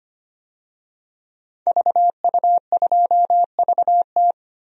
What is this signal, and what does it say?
Morse code, heard as a keyed tone, about 25 wpm, 700 Hz. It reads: VU2VT
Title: DXCC-100 Call Signs - 5 Characters - 25wpm